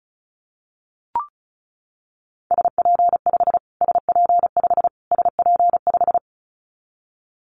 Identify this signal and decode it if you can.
Morse code, heard as a keyed tone, about 34 wpm, 700 Hz. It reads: SP5 SP5 SP5
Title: Callsign Prefixes for Top 25 DXCC - Repeated 3x - 35wpm